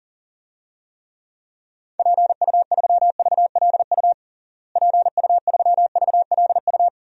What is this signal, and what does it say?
Morse code, heard as a keyed tone, about 40 wpm, 700 Hz. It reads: PU3VLU PU3VLU
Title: DXCC-100 Call Signs - 6 Characters - Repeated 2x - 40wpm